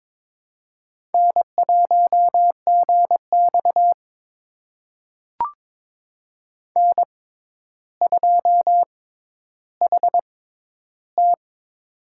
Morse code, heard as a keyed tone, about 21 wpm, spaced as for 11 wpm, 700 Hz. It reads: N1GX N2HT
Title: USA Call Signs - 4 Characters - 6x Character Spacing - 22wpm